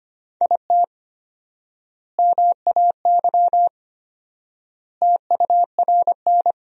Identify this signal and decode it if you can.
Morse code, heard as a keyed tone, about 25 wpm, 700 Hz. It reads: IT MAY TURN
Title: 4x Word-Spacing - Sentences from Top 400 Words - 25wpm